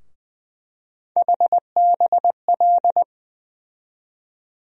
Morse code, heard as a keyed tone, about 20 wpm, 700 Hz. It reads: HBL
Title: USA Call Sign Postfixes - 3 Letters 20wpm